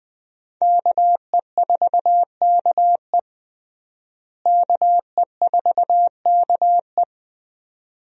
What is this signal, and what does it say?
Morse code, heard as a keyed tone, about 20 wpm, 700 Hz. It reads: KE4KE KE4KE
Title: USA Call Signs - 5 Characters - Repeated 2x - 20wpm